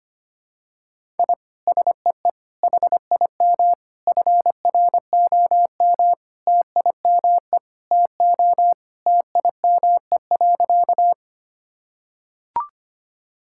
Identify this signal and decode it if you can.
Morse code, heard as a keyed tone, about 25 wpm, 700 Hz. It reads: I SEE HIM FROM TIME TO TIME.
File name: practice / 25wpm / Sentences with 4-Word Phrases - 25wpm